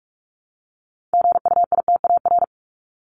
Morse code, heard as a keyed tone, about 45 wpm, 700 Hz. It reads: GUITAR
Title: Top 1501-1750 Words 45wpm